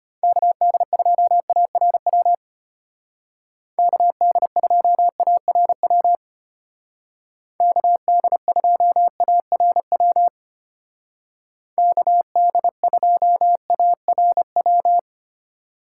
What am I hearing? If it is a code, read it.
Morse code, heard as a keyed tone, about 30 wpm, 700 Hz. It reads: KD2ARW KD2ARW KD2ARW KD2ARW
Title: Speed Racing - USA 2x3 Call Signs 25wpm